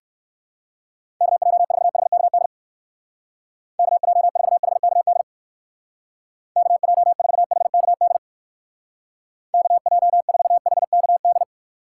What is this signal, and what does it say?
Morse code, heard as a keyed tone, about 56 wpm, 700 Hz. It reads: KJ4HKD KJ4HKD KJ4HKD KJ4HKD
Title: Speed Racing - USA 2x3 Call Signs 45wpm